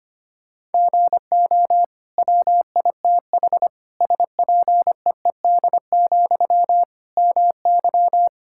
Morse code, heard as a keyed tone, about 25 wpm, 700 Hz. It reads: GO WITH SPEED, MY
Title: Sentences from Top 1000 Words - Review Part 5 - 25wpm